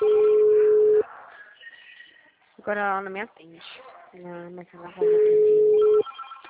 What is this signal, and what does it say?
Ringback tone (Europe, 425 Hz), with speech.
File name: out- -2041- - - .30197